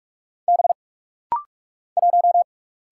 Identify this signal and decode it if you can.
Morse code, heard as a keyed tone, about 44 wpm, 700 Hz. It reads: B 1
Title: Single Letter-Number - Rapid-Fire 45wpm